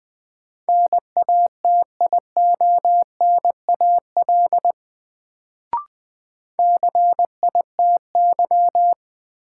Morse code, heard as a keyed tone, about 20 wpm, 700 Hz. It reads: NATIONAL CITY
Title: Sets of 2 Words - New Word Drill - 20wpm